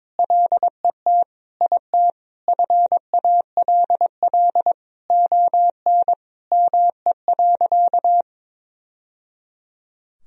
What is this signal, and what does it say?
Morse code, heard as a keyed tone, about 21 wpm, 700 Hz. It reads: LET IT FALL ON ME.
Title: Sentences from Top 1000 Words - Review Part 2 - 22wpm